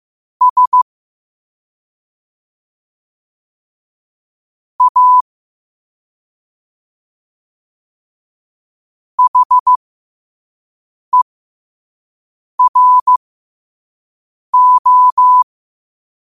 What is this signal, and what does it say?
Morse code, heard as a keyed tone, about 15 wpm, spaced as for 5 wpm, 1000 Hz. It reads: S A HERO
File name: qotdfiles / 5wpm